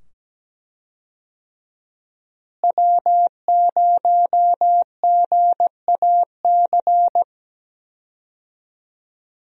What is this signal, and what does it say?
Morse code, heard as a keyed tone, about 17 wpm, 700 Hz. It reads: W0GAC
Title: USA Call Signs - 5 Characters - Repeated 2x - 17wpm